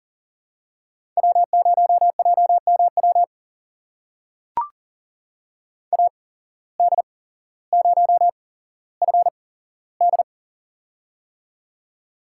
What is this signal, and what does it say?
Morse code, heard as a keyed tone, about 38 wpm, spaced as for 17 wpm, 700 Hz. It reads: W0JMW AD0FD
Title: USA Call Signs - 5 Characters - 8x Character Spacing - 40wpm